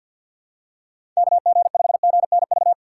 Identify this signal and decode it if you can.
Morse code, heard as a keyed tone, about 47 wpm, 700 Hz. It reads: KG5GNV
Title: USA Call Signs - 6 Characters - 6x Character Spacing - 50wpm